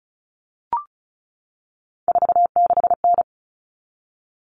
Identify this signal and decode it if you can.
Morse code, heard as a keyed tone, about 34 wpm, 700 Hz. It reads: 46N
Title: All Cut Numbers - 3-Digit 35wpm